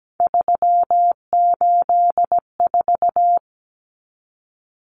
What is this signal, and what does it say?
Morse code, heard as a keyed tone, about 17 wpm, 700 Hz. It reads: 384
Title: Numbers - 3-Digit 17wpm